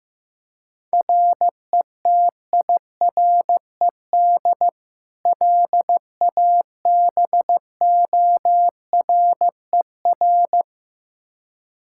Morse code, heard as a keyed tone, about 15 wpm, 700 Hz. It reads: RETIRED LABORER
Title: Practice QSO - Exchange 3 - More Info - 15wpm